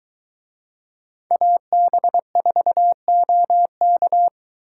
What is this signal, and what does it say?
Morse code, heard as a keyed tone, about 22 wpm, 700 Hz. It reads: AB4OK